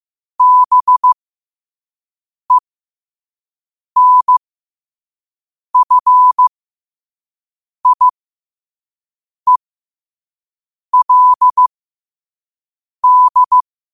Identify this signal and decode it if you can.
Morse code, heard as a keyed tone, about 15 wpm, spaced as for 5 wpm, 1000 Hz. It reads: BENFIELD
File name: qotdfiles / 5wpm